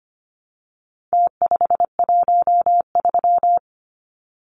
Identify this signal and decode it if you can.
Morse code, heard as a keyed tone, about 25 wpm, 700 Hz. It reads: T513
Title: Numbers - Contest Style - 4-Digit 25wpm